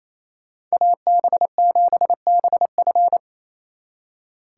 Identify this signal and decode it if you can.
Morse code, heard as a keyed tone, about 34 wpm, 700 Hz. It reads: AB7BF
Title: Speed Racing - USA 2x2 Call Signs 28wpm